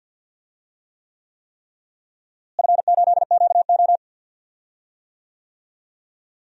Morse code, heard as a keyed tone, about 53 wpm, 700 Hz. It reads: F8QO
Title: DXCC-100 Call Signs - 4 Characters - Repeated 3x - 50wpm